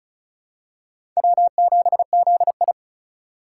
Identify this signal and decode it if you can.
Morse code, heard as a keyed tone, about 35 wpm, 700 Hz. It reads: W7ZI
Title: USA Call Signs - 4 Characters - 8x Character Spacing - 35wpm